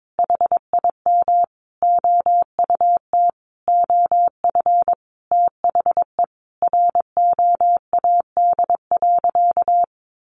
Morse code, heard as a keyed tone, about 22 wpm, 700 Hz. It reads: HIM OUT OF THE ROAD.
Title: Sentences from Top 400 Words 22wpm